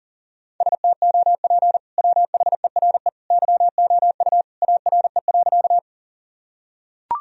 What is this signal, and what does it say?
Morse code, heard as a keyed tone, about 40 wpm, 700 Hz. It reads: STOP WHERE YOU ARE.
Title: Sentences from Top 1000 Words - Review Part 1 - 40wpm